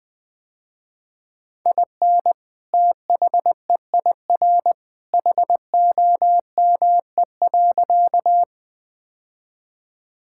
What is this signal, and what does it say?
Morse code, heard as a keyed tone, about 20 wpm, 700 Hz. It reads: IN THEIR HOME.
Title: Sentences with 2-Word Phrases - Spoken Beginning - 20wpm